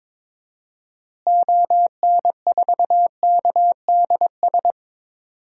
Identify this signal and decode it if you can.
Morse code, heard as a keyed tone, about 22 wpm, 700 Hz. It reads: ON4KDS
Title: DXCC-100 Call Signs - 6 Characters - Repeated 3x - 22wpm